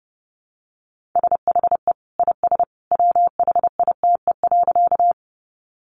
Morse code, heard as a keyed tone, about 28 wpm, 700 Hz. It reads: SHE IS WHITE.
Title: Easy Phrases 30wpm